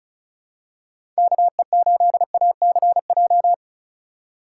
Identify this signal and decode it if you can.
Morse code, heard as a keyed tone, about 36 wpm, 700 Hz. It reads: KE8ACJ
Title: USA Call Signs - 6 Characters - 2x Character Spacing - 35wpm